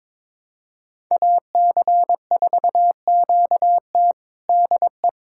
Morse code, heard as a keyed tone, about 21 wpm, 700 Hz. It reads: AC4QT DE
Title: USA Callsign DE Callsign - 5 Characters - 22wpm